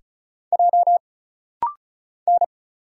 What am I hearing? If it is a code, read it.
Morse code, heard as a keyed tone, about 34 wpm, 700 Hz. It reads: J N
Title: Single Letter-Number Alt - Rapid-Fire 35wpm